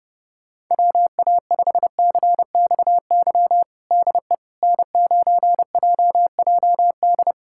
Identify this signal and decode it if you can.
Morse code, heard as a keyed tone, about 32 wpm, 700 Hz. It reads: WA5CXY DE N9JJD
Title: USA Callsign DE Callsign 30wpm